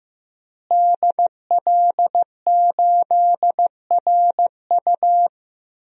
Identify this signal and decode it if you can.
Morse code, heard as a keyed tone, about 15 wpm, 700 Hz. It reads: DL8RU
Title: DXCC-100 Call Signs - 5 Characters - Repeated 2x - 15wpm